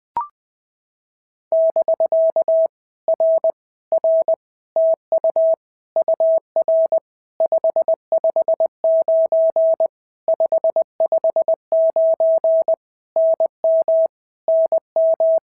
Morse code, heard as a keyed tone, about 20 wpm, 650 Hz. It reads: <BK> R R TU UR 559 559 NM NM
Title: POTA QSO Practice - 20wpm